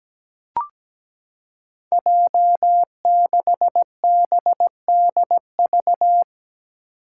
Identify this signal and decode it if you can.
Morse code, heard as a keyed tone, about 17 wpm, 700 Hz. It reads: J6BDV
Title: Five Challenging Letter-Number Combinations 17wpm